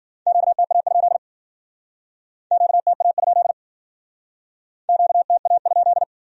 Morse code, heard as a keyed tone, about 49 wpm, 700 Hz. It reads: QTA? QTA? QTA?
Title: Speed-Racing Q-Codes - 40wpm